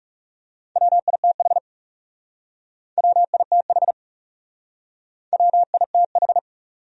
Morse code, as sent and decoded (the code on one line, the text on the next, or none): .-- .. - .... / .-- .. - .... / .-- .. - ....
WITH WITH WITH